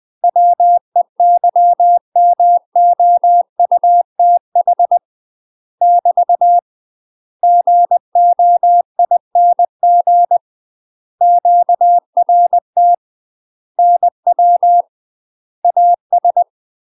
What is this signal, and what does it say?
Morse code, heard as a keyed tone, about 20 wpm, 700 Hz. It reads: WEYMOUTH <BT> GOING QRT NW AS
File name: UK MASTER QSOs TRAINING / 20WPM / UK MASTER QSOs 0044